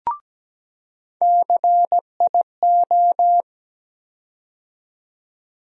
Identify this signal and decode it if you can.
Morse code, heard as a keyed tone, about 17 wpm, 700 Hz. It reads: CIO
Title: USA Call Sign Postfixes - 3 Letters 17wpm